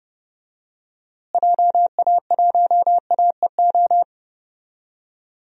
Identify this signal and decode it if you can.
Morse code, heard as a keyed tone, about 31 wpm, 700 Hz. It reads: JA1AEO
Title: DXCC-100 Call Signs - 6 Characters - Repeated 2x - 30wpm